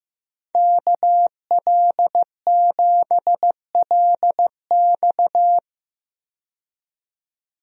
Morse code, heard as a keyed tone, about 15 wpm, 700 Hz. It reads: KL7LX
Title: DXCC-100 Call Signs - 5 Characters - 15wpm